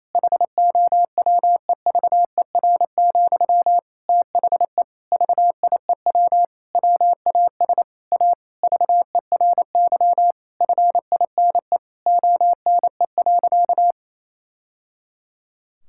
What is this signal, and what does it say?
Morse code, heard as a keyed tone, about 27 wpm, 700 Hz. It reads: HOWEVER, THE VIEW WAS A VERY FINE ONE.
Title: Sentences from Top 1000 Words - Review Part 6 - 28wpm